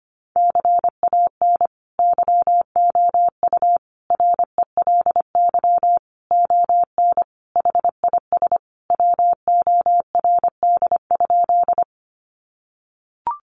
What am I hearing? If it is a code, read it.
Morse code, heard as a keyed tone, about 25 wpm, 700 Hz. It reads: CAN YOU RELY ON HIS WORD?
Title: Sentences from Top 1200 Words 25wpm